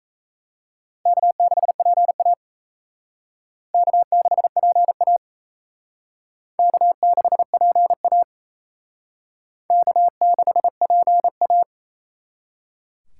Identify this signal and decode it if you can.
Morse code, heard as a keyed tone, about 34 wpm, 700 Hz. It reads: K6PA K6PA K6PA K6PA
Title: Speed Racing - USA Call Signs 28wpm